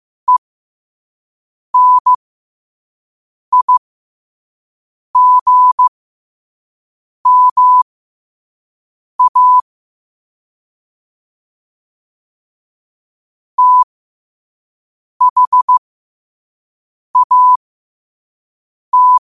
Morse code, heard as a keyed tone, about 15 wpm, spaced as for 5 wpm, 1000 Hz. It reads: ENIGMA THAT
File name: 5wpm